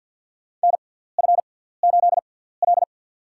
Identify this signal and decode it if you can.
Morse code, heard as a keyed tone, about 52 wpm, spaced as for 17 wpm, 700 Hz. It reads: NF8L
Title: USA Call Signs - 6x Character Spacing - 50wpm